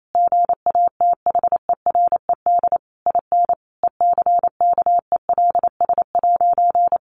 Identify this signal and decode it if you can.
Morse code, heard as a keyed tone, about 28 wpm, 700 Hz. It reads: GATHERED IN ECKELS'
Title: A Sound Of Thunder 28wpm